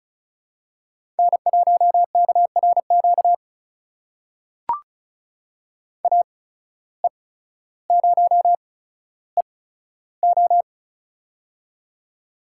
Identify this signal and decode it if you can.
Morse code, heard as a keyed tone, about 34 wpm, spaced as for 15 wpm, 700 Hz. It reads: N1KRQ AE0EO
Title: USA Call Signs - 5 Characters - 8x Character Spacing - 35wpm